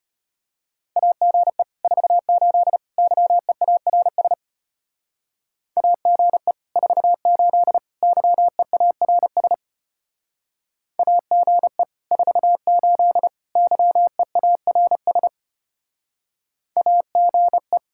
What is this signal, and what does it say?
Morse code, heard as a keyed tone, about 33 wpm, 700 Hz. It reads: AGE 48 YEARS AGE 48 YEARS AGE 48 YEARS AGE